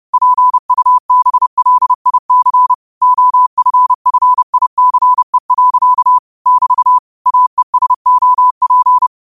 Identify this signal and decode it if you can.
Morse code, heard as a keyed tone, about 30 wpm, 1000 Hz. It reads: PUBLIC OFFICE. <BT> AESOP
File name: qotdfiles / 30wpm